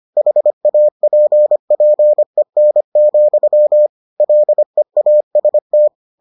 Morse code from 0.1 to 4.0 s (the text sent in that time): HAPPEN,